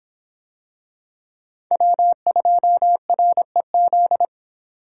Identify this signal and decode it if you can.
Morse code, heard as a keyed tone, about 25 wpm, 700 Hz. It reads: W2REZ